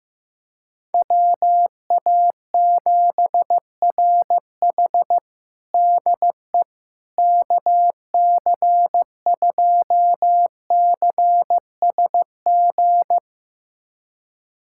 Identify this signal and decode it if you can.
Morse code, heard as a keyed tone, about 15 wpm, 700 Hz. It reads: WA7RH DE KC2CSG
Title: USA Callsign DE Callsign 15wpm